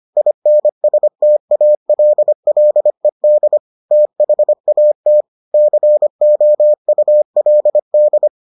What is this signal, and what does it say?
Morse code, heard as a keyed tone, about 25 wpm, 600 Hz. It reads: INSTALLED THAT COULD